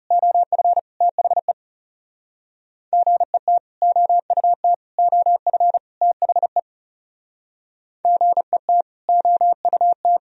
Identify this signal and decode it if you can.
Morse code, heard as a keyed tone, about 34 wpm, 700 Hz. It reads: OF THE GET OUT OF THE GET OUT
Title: Speed-Racing - Sets of 4 Words 30wpm